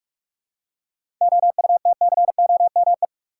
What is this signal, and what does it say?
Morse code, heard as a keyed tone, about 46 wpm, 700 Hz. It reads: OUTCOME
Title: Seven Letter Words - 45wpm